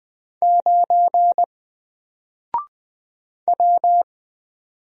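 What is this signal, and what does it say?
Morse code, heard as a keyed tone, about 20 wpm, 700 Hz. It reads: 9 W
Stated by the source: Single letter and number Morse drill at 20 wpm, from N4LCD.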